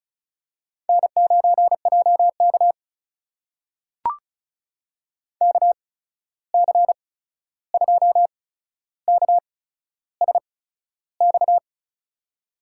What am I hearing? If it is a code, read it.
Morse code, heard as a keyed tone, about 36 wpm, spaced as for 13 wpm, 700 Hz. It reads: N9JK KC2KSX
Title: USA Call Signs - 8x Character Spacing - 35wpm